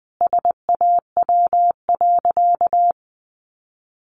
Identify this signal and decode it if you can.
Morse code, heard as a keyed tone, about 20 wpm, 700 Hz. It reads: SAW.
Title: Sentences from Top 300 Words 20wpm